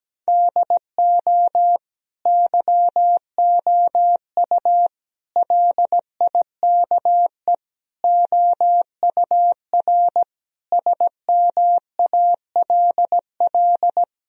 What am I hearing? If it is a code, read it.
Morse code, heard as a keyed tone, about 17 wpm, 700 Hz. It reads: DO YOU LIKE OUR SMALL
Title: Sentences from Top 1000 Words - Review Part 1 - 17wpm